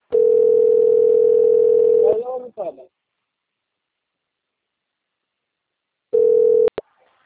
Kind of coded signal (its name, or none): telephone call progress tone